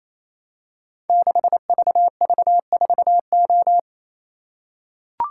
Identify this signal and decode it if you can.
Morse code, heard as a keyed tone, about 27 wpm, 700 Hz. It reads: 6VV4O
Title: Code Groups - 5 Letter:Numbers - 28wpm